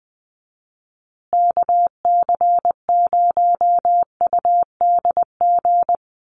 Morse code, heard as a keyed tone, about 20 wpm, 700 Hz. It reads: KC0UDG
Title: USA Call Signs Repeated 3x 20wpm